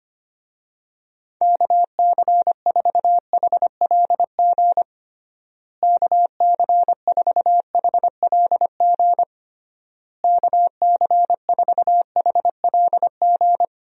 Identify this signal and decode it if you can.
Morse code, heard as a keyed tone, about 25 wpm, 700 Hz. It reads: KC4HLG KC4HLG KC4HLG
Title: USA Call Signs - 6 Characters - Repeated 3x - 25wpm